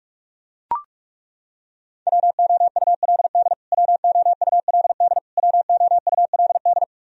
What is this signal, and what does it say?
Morse code, heard as a keyed tone, about 42 wpm, 700 Hz. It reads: WOULD WOULD WOULD
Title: Top 100 Words Repeated 45wpm